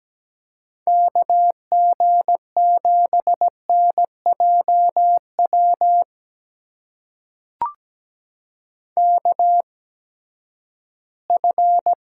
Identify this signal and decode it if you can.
Morse code, heard as a keyed tone, about 17 wpm, spaced as for 12 wpm, 700 Hz. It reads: KG7NJW KF
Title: USA Call Signs - 6 Characters - 8x Character Spacing - 17wpm